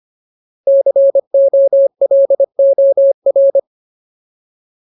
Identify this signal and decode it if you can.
Morse code, heard as a keyed tone, about 25 wpm, 550 Hz. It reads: COLOR
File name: Ultimate Word List / 25WPM Ultimate